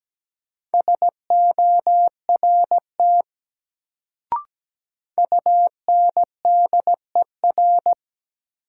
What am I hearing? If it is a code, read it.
Morse code, heard as a keyed tone, about 17 wpm, 700 Hz. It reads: SORT UNDER
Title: Sets of 2 Words - New Word Drill - 17wpm